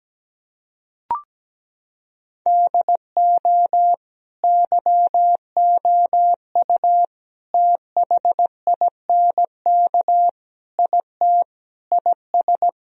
Morse code, heard as a keyed tone, about 17 wpm, 700 Hz. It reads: DO YOU THINK IT IS
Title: Sentences from Top 1000 Words - Review Part 1 - 17wpm